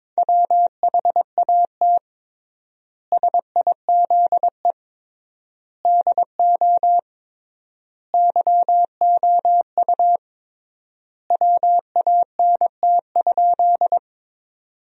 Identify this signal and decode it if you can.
Morse code, heard as a keyed tone, about 22 wpm, 700 Hz. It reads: WHAT SIZE DO YOU WANT?
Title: 3x Word-Spacing - Sentences from Top 400 Words - 22wpm